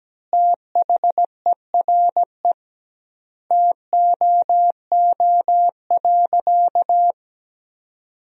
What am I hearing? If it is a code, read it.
Morse code, heard as a keyed tone, about 17 wpm, 700 Hz. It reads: THERE TOO.
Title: 2x Word-Spacing - Sentences from Top 200 Words - 17wpm